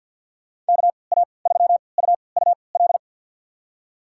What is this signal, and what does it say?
Morse code, heard as a keyed tone, about 51 wpm, spaced as for 28 wpm, 690 Hz. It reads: KA3UUL
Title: USA Call Signs - 6 Characters - 3x Character Spacing - 50wpm